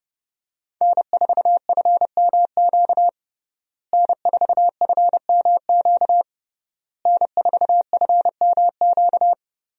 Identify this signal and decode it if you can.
Morse code, heard as a keyed tone, about 30 wpm, 700 Hz. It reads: N4FMQ N4FMQ N4FMQ
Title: USA Call Signs - 5 Characters - Repeated 3x - 30wpm